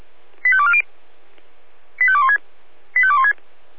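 Five-tone selective call (ZVEI), heard as quick runs of five tones, tones of about 70 ms. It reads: 85240 85216 85216
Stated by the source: ZVEI five-tone selective call received with an SDR.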